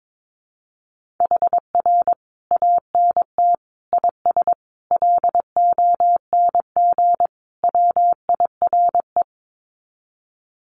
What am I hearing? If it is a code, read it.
Morse code, heard as a keyed tone, about 22 wpm, 700 Hz. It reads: HR ANT IS LONG WIRE
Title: Common QSO Phrases 22wpm